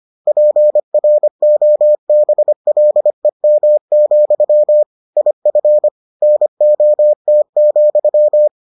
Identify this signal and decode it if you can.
Morse code, heard as a keyed tone, about 25 wpm, 600 Hz. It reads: PROBLEM, IF NOT,